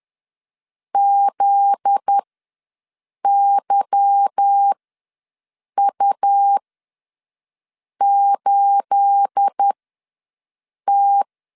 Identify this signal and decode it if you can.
Morse code, heard as a keyed tone, about 10 wpm, spaced as for 5 wpm, 800 Hz. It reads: ZYU8T
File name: LEZIONI / 9 28 ws 10wpm 4124